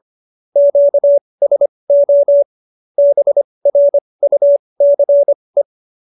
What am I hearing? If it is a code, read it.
Morse code, heard as a keyed tone, about 25 wpm, spaced as for 20 wpm, 570 Hz. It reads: QSO BRUCE